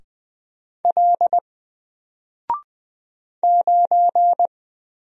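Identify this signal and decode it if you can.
Morse code, heard as a keyed tone, about 20 wpm, 700 Hz. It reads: L 9
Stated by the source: Single letter and number Morse drill at 20 wpm, from N4LCD.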